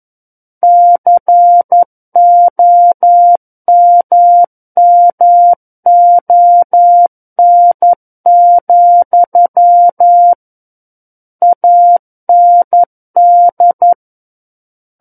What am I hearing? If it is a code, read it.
Morse code, heard as a keyed tone, about 11 wpm, 700 Hz. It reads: COMMON, AND
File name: 11 WPM